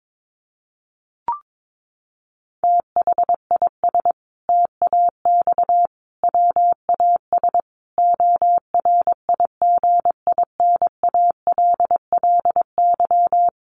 Morse code, heard as a keyed tone, about 22 wpm, 700 Hz. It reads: THIS TAX WAS ORIGINALLY
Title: Sentences from Top 1500 Words 22wpm